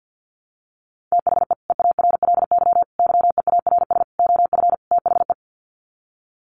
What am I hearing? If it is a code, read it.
Morse code, heard as a keyed tone, about 49 wpm, 700 Hz. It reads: THE EARLY YEARS OF THE
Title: Sets of 5 Words Encore 50wpm